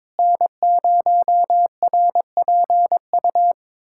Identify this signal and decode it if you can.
Morse code, heard as a keyed tone, about 22 wpm, 700 Hz. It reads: N0RPU
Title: USA Callsign DE Callsign 22wpm